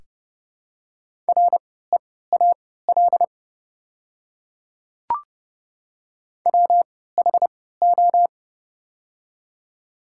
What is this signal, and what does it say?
Morse code, heard as a keyed tone, about 30 wpm, spaced as for 17 wpm, 690 Hz. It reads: REAL WHO
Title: 3x Character Spacing - Top 300 Words - 30wpm